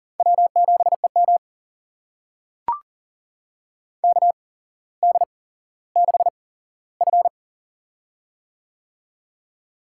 Morse code, heard as a keyed tone, about 40 wpm, spaced as for 17 wpm, 700 Hz. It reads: W7EM KD6F
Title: USA Call Signs - 4 Characters - 8x Character Spacing - 40wpm